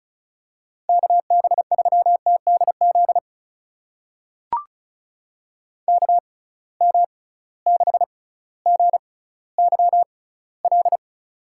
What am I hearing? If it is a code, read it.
Morse code, heard as a keyed tone, about 34 wpm, spaced as for 18 wpm, 700 Hz. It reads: KB3TDZ KM6GYL
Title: USA Call Signs - 6 Characters - 6x Character Spacing - 35wpm